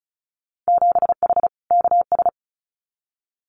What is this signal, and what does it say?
Morse code, heard as a keyed tone, about 36 wpm, 700 Hz. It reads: 7H KS
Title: Winter Field Day Exchanges Encore 35wpm